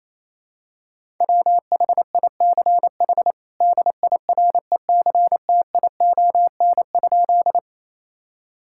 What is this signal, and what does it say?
Morse code, heard as a keyed tone, about 28 wpm, 700 Hz. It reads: WHICH DIRECTION?
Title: Sentences from Top 1000 Words - Review Part 5 - 28wpm